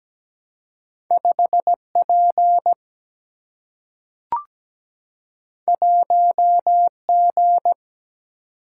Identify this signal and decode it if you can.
Morse code, heard as a keyed tone, about 17 wpm, 700 Hz. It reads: 5P 1G
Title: Single-Letter Single-Number 17wpm